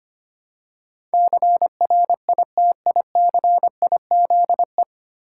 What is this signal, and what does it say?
Morse code, heard as a keyed tone, about 25 wpm, 690 Hz. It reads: CRITICIZE
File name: practice / 25wpm / 9 Letter Words - First 3 Letters Given - 25wpm